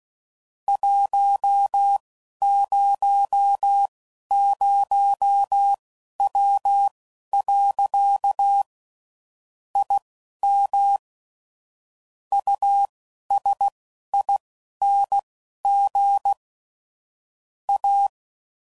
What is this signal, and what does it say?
Morse code, heard as a keyed tone, about 16 wpm, spaced as for 11 wpm, 800 Hz. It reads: 100W. IM USING A
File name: CW Code Groups @15WPM / QSOs / WD0FBT Spaced